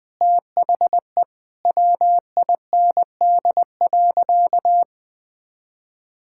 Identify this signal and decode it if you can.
Morse code, heard as a keyed tone, about 20 wpm, 700 Hz. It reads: THE WIND.
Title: Sentences with 2-Word Phrases - 20wpm